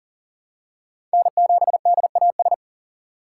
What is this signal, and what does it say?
Morse code, heard as a keyed tone, about 40 wpm, 700 Hz. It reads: N7DAS
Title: USA Call Signs - 5 Characters - 2x Character Spacing - 40wpm